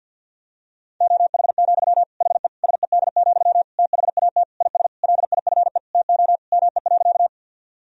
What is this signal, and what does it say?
Morse code, heard as a keyed tone, about 50 wpm, 700 Hz. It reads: OH, HE SED, THAT IS LIFE TO ME.
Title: Sentences from Top 500 Words and Abbreviations 50wpm